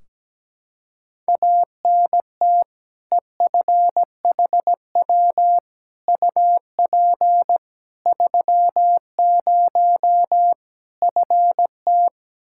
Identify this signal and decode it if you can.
Morse code, heard as a keyed tone, about 17 wpm, 700 Hz. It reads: ANT EFHW UP 30 FT
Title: Practice QSO - Exchange 2 - Info - 17wpm